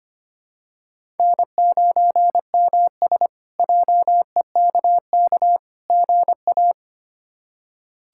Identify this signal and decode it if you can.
Morse code, heard as a keyed tone, about 25 wpm, 700 Hz. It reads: N9MS JEKK GA
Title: K1USN SST QSO Practice - 25wpm